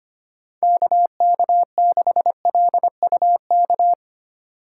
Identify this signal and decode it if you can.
Morse code, heard as a keyed tone, about 25 wpm, 700 Hz. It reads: KK6LUK
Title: USA Call Signs - 6 Characters - 4x Character Spacing - 25wpm